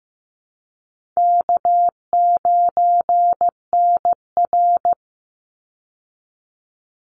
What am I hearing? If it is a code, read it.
Morse code, heard as a keyed tone, about 15 wpm, 700 Hz. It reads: K9NR
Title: Easy Call Signs - 15wpm